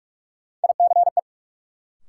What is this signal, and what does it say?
Morse code, heard as a keyed tone, about 44 wpm, 700 Hz. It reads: ICE
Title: Sentences from Top 1000 Words - New Word Drill - 45wpm